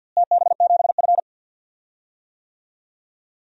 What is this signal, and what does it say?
Morse code, heard as a keyed tone, about 52 wpm, 700 Hz. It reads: TB7F
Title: DXCC-100 Call Signs - 4 Characters - Repeated 3x - 50wpm